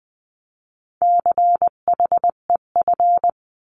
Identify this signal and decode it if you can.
Morse code, heard as a keyed tone, about 20 wpm, 700 Hz. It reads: CHEF
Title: 259 Lesson 42d - Words 18-20wpm